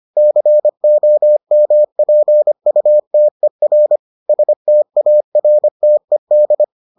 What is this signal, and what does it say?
Morse code, heard as a keyed tone, about 25 wpm, 600 Hz. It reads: COMPUTER STARTED